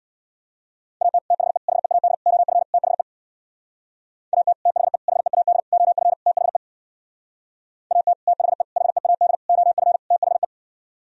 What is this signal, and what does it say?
Morse code, heard as a keyed tone, about 62 wpm, 700 Hz. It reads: AT THE HEAD OF THE AT THE HEAD OF THE AT THE HEAD OF THE
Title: Speed-Racing - Sets of 5 Words 50wpm